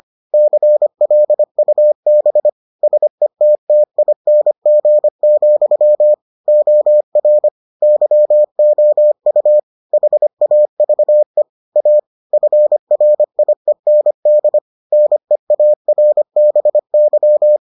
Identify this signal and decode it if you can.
Morse code, heard as a keyed tone, about 25 wpm, 600 Hz. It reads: CLUB SETTING, OR YOU HAVE A FRIEND NEARBY